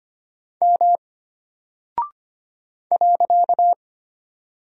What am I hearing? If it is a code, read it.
Morse code, heard as a keyed tone, about 25 wpm, 700 Hz. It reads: M .
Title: Single Character (Letter:Number:Punctuation:Prosign) Alt 25wpm